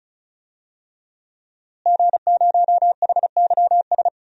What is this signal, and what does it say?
Morse code, heard as a keyed tone, about 34 wpm, 700 Hz. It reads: G0HYS
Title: DXCC-100 Call Signs - 5 Characters - Repeated 3x - 35wpm